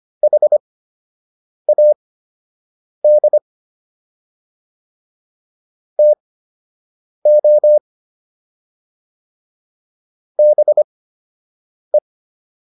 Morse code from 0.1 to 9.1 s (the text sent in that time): HAD TO